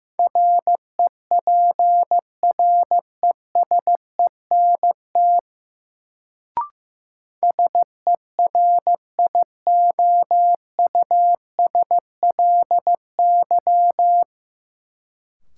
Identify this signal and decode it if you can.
Morse code, heard as a keyed tone, about 15 wpm, 700 Hz. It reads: REPRESENT SERIOUSLY
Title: Sentences from Top 1200 Words - New Word Drill Alt - 15wpm